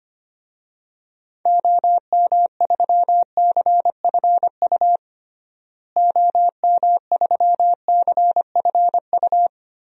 Morse code, heard as a keyed tone, about 25 wpm, 700 Hz. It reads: OM3CFU OM3CFU
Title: DXCC-100 Call Signs - 6 Characters - Repeated 3x - 25wpm